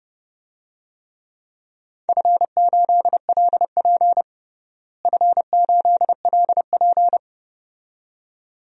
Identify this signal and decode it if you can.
Morse code, heard as a keyed tone, about 32 wpm, 700 Hz. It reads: F8LP F8LP
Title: DXCC-100 Call Signs - 4 Characters - Repeated 2x - 30wpm